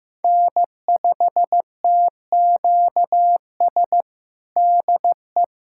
Morse code, heard as a keyed tone, about 16 wpm, 700 Hz. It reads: N5TQS DE
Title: USA Callsign DE Callsign - 5 Characters - 15wpm